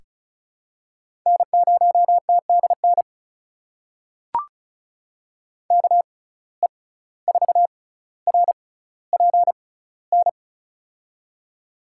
Morse code, heard as a keyed tone, about 36 wpm, spaced as for 17 wpm, 700 Hz. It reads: N0TDN KE4RPN
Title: USA Call Signs - 6x Character Spacing - 35wpm